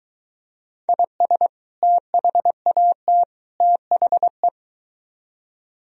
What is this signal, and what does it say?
Morse code, heard as a keyed tone, about 23 wpm, 700 Hz. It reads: IS THAT THE